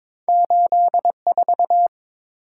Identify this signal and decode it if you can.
Morse code, heard as a keyed tone, about 22 wpm, 700 Hz. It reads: 84
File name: practice / 22wpm / Two Letter-Number Combinations - 22wpm